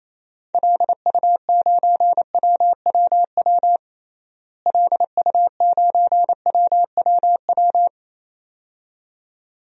Morse code, heard as a keyed tone, about 28 wpm, 700 Hz. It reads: LU9WWW LU9WWW
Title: DXCC-100 Call Signs - 6 Characters - Repeated 2x - 28wpm